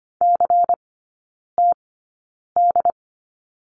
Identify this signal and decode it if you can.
Morse code, heard as a keyed tone, about 25 wpm, spaced as for 9 wpm, 700 Hz. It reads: CTD
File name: Lesson 17e - Letter u - Three Characters 25wpm